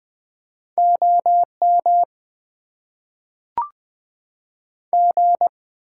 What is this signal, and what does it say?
Morse code, heard as a keyed tone, about 20 wpm, 700 Hz. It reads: OM G
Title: Top 90 QSO Elements - 8x Character Spacing - 20wpm